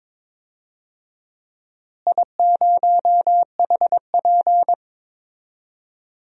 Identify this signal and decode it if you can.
Morse code, heard as a keyed tone, about 22 wpm, 700 Hz. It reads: I0HP
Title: DXCC-100 Call Signs - 4 Characters - Repeated 3x - 22wpm